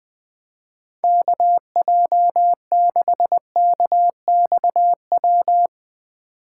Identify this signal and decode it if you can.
Morse code, heard as a keyed tone, about 22 wpm, 700 Hz. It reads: KJ6KXW
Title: Speed Racing - USA 2x3 Call Signs 20wpm